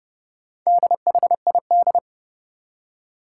Morse code, heard as a keyed tone, about 32 wpm, 700 Hz. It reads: DHID
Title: Four Challenging Letter-Number Combinations 30wpm